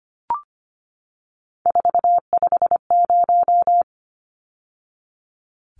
Morse code, heard as a keyed tone, about 25 wpm, 700 Hz. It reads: 450
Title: Numbers - 3-Digit 25wpm